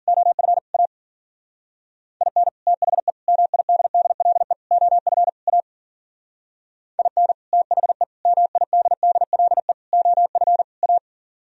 Morse code, heard as a keyed tone, about 48 wpm, 700 Hz. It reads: OF A IN THE MIDDLE OF A IN THE MIDDLE OF A
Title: Speed-Racing - Sets of 5 Words 40wpm